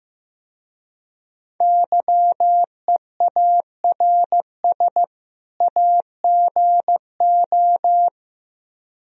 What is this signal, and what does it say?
Morse code, heard as a keyed tone, about 17 wpm, 700 Hz. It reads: YEARS AGO
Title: Speed-Racing - Sets of 2 Words 15wpm